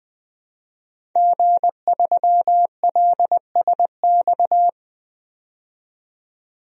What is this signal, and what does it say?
Morse code, heard as a keyed tone, about 20 wpm, 700 Hz. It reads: G3LSX
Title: DXCC-100 Call Signs - 5 Characters - 20wpm